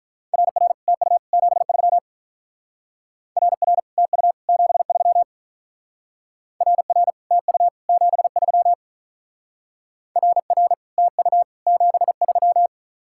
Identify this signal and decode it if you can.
Morse code, heard as a keyed tone, about 43 wpm, 700 Hz. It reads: RR TU 73 RR TU 73 RR TU 73 RR TU 73
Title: Speed-Racing Common QSO Phrases - 35wpm